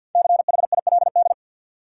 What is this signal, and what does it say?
Morse code, heard as a keyed tone, about 53 wpm, 700 Hz. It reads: CHILD
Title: Sentences from Top 800 Words - New Word Drill - 50wpm